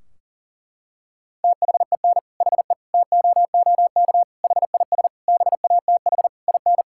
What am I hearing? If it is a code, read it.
Morse code, heard as a keyed tone, about 40 wpm, 700 Hz. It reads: THEN HE TOOK HIS BATH IN
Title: Sentences from Top 1600 Words 40wpm